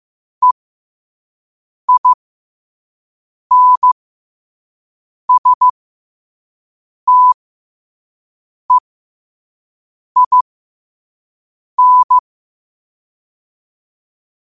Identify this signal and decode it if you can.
Morse code, heard as a keyed tone, about 15 wpm, spaced as for 5 wpm, 1000 Hz. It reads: EINSTEIN
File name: qotdfiles / 5wpm